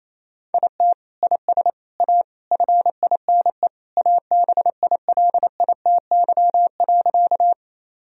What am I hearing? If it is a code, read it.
Morse code, heard as a keyed tone, about 28 wpm, 700 Hz. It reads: IT IS A FINE ABILITY.
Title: Sentences from Top 1000 Words - Review Part 3 - 28wpm